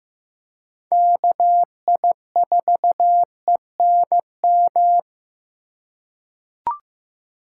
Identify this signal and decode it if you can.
Morse code, heard as a keyed tone, about 15 wpm, 700 Hz. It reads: KI4ENM
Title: USA Call Signs - 6 Characters - 6x Character Spacing - 15wpm